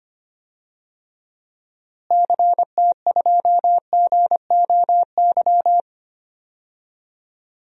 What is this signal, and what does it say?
Morse code, heard as a keyed tone, about 25 wpm, 700 Hz. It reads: CT2GOY
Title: DXCC-100 Call Signs - 6 Characters - Repeated 2x - 25wpm